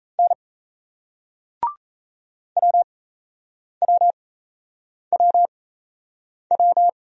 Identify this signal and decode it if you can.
Morse code, heard as a keyed tone, about 35 wpm, spaced as for 8 wpm, 700 Hz. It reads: N WWWW
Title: Speed-Racing Single Letters 28wpm